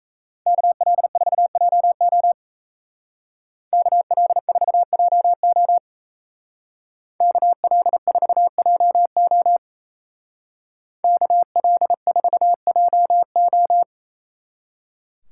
Morse code, heard as a keyed tone, about 34 wpm, 700 Hz. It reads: KL4JO KL4JO KL4JO KL4JO
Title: Speed Racing - USA 2x2 Call Signs 28wpm